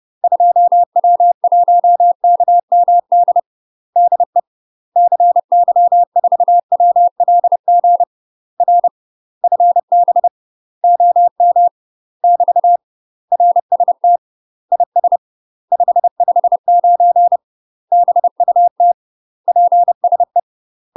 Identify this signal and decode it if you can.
Morse code, heard as a keyed tone, about 30 wpm, 700 Hz. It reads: 2W1KMD DE CY4WLG R FB OM <BT> RST IS 559 BUT PSE
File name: UK MASTER QSOs TRAINING / 30WPM / UK MASTER QSOs 0038